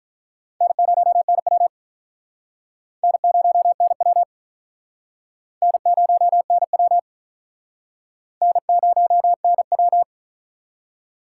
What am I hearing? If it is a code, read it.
Morse code, heard as a keyed tone, about 46 wpm, 700 Hz. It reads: N0NW N0NW N0NW N0NW